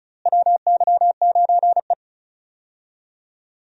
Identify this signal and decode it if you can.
Morse code, heard as a keyed tone, about 36 wpm, 700 Hz. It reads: WY9E
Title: USA Call Signs - 4 Characters - Repeated 2x - 35wpm